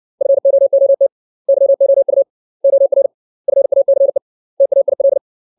Morse code, heard as a keyed tone, about 62 wpm, 550 Hz. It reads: FOOT YOU OR FACE TAIL
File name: 632 words / 632 Words 60WPM